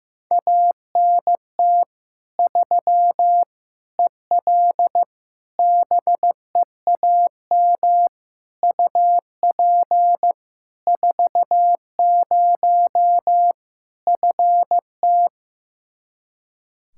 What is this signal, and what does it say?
Morse code, heard as a keyed tone, about 15 wpm, 700 Hz. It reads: ANT 3 EL BEAM UP 40 FT
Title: Practice QSO - Exchange 2 - Info - 15wpm